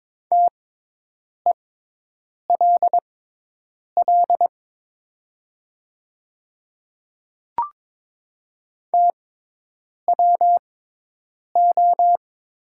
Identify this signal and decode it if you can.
Morse code, heard as a keyed tone, about 22 wpm, spaced as for 8 wpm, 700 Hz. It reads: TELL TWO
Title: 6x Character Spacing - Top 200 Words - 22wpm